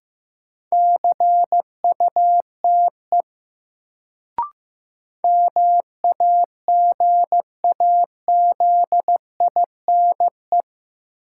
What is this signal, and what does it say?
Morse code, heard as a keyed tone, about 15 wpm, 700 Hz. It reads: CUTE MAGAZINE
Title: Sentences from Top 1000 Words - Review Part 7 - New Word Drill - 15wpm